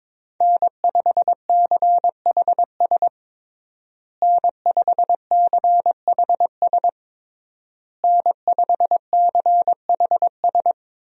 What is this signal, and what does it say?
Morse code, heard as a keyed tone, about 21 wpm, 700 Hz. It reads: N5CHS N5CHS N5CHS
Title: USA Call Signs - 5 Characters - Repeated 3x - 22wpm